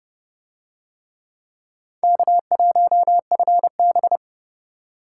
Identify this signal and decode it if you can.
Morse code, heard as a keyed tone, about 32 wpm, 700 Hz. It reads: K1FB